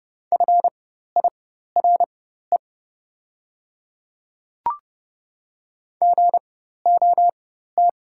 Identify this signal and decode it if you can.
Morse code, heard as a keyed tone, about 30 wpm, spaced as for 14 wpm, 700 Hz. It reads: FIRE GOT
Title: 4x Character Spacing - Top 500 Words - 30wpm